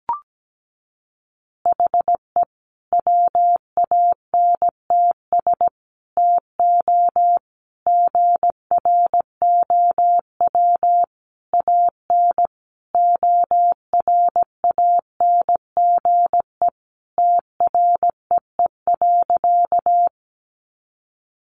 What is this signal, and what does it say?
Morse code, heard as a keyed tone, about 17 wpm, 700 Hz. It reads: HE WANTS TO GROW AN ORANGE TREE.
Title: Easy Phrases 17wpm